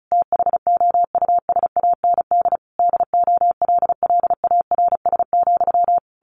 Morse code, heard as a keyed tone, about 34 wpm, 700 Hz. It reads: THOUSAND DOLLARS,
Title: A Sound Of Thunder 35wpm